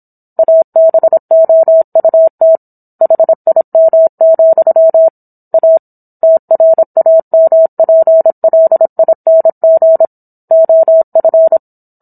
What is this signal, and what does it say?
Morse code, heard as a keyed tone, about 27 wpm, 650 Hz. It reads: ABOUT HIM, A TRAMPLING OF
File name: Book One / 17-Chapter 17 - 26 WPM